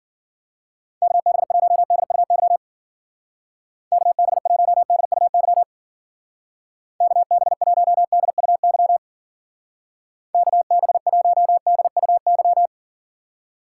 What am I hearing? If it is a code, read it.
Morse code, heard as a keyed tone, about 48 wpm, 700 Hz. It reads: KB1DUY KB1DUY KB1DUY KB1DUY
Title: Speed Racing - USA 2x3 Call Signs 40wpm